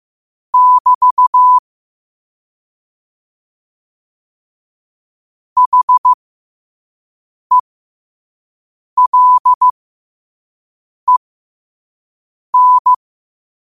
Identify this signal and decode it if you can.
Morse code, heard as a keyed tone, about 15 wpm, spaced as for 5 wpm, 1000 Hz. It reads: <BT> HELEN
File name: qotdfiles / 5wpm